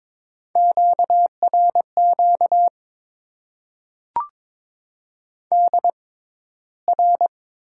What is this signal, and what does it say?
Morse code, heard as a keyed tone, about 22 wpm, spaced as for 13 wpm, 700 Hz. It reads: QRQ DR
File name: Top 90 QSO Elements - 6x Character Spacing - 22wpm